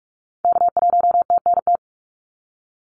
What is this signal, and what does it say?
Morse code, heard as a keyed tone, about 46 wpm, 700 Hz. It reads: K1TNT
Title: NA SOTA Activator Callsigns 45wpm